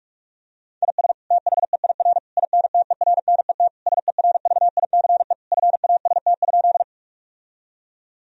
Morse code, heard as a keyed tone, about 46 wpm, 700 Hz. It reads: IS THEIR INTERNET SERVICE FAST?
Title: Sentences from Top 1000 Words - Review Part 4 - 45wpm